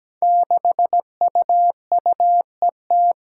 Morse code, heard as a keyed tone, about 17 wpm, 700 Hz. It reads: 6UUET
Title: Five Challenging Letter-Number Combinations 17wpm